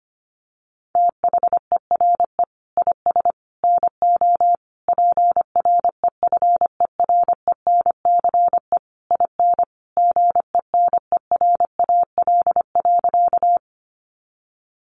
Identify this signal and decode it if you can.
Morse code, heard as a keyed tone, about 25 wpm, 700 Hz. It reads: THERE IS NO PREFERENCE IN GENERAL.
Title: Sentences from Top 1500 Words 25wpm